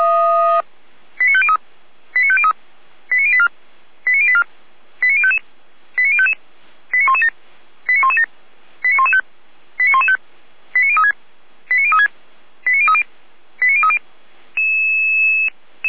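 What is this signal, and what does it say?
Five-tone selective call (ZVEI), heard as quick runs of five tones, tones of about 70 ms. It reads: 89503 89503 89084 89084 89055 89055 89117 89117 89115 89115 89036 89036 89030 89030
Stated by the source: Five-tone ZVEI selective call received by SDR.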